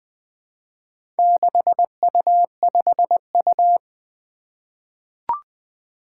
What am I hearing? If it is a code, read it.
Morse code, heard as a keyed tone, about 20 wpm, 700 Hz. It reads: 6U5U
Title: Four Challenging Letter-Number Combinations 20wpm